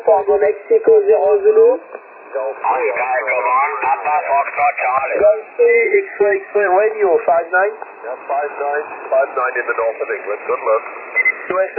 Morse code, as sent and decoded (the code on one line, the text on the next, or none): none